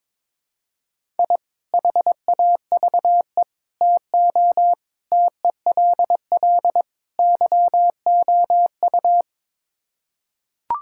Morse code, heard as a keyed tone, about 22 wpm, 700 Hz. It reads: I HAVE TO TELL YOU
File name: practice / 22wpm / Build A Phrase - 5-Word Phrases - 22wpm